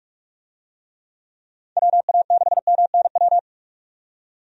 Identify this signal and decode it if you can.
Morse code, heard as a keyed tone, about 46 wpm, 700 Hz. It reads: WA6MNW
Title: USA Call Signs - 6 Characters - 45wpm